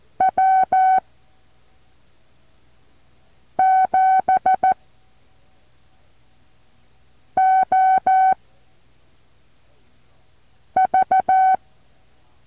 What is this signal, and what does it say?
Morse code, heard as a keyed tone, about 14 wpm, spaced as for 3 wpm, 760 Hz. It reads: W7OV
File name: intl morse code2